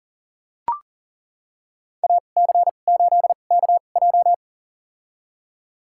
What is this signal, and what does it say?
Morse code, heard as a keyed tone, about 37 wpm, spaced as for 28 wpm, 700 Hz. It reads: AC8KJ
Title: USA Call Signs - 5 Characters - 2x Character Spacing - 40wpm